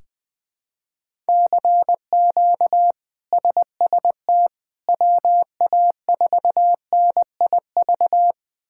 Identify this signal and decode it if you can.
Morse code, heard as a keyed tone, about 20 wpm, 700 Hz. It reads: CQ SST WA4NIV
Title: K1USN SST QSO Practice - 20wpm